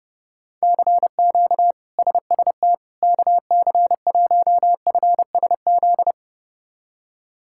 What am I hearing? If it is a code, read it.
Morse code, heard as a keyed tone, about 30 wpm, 700 Hz. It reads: CQ SST KC1FSZ
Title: K1USN SST QSO Practice - 30wpm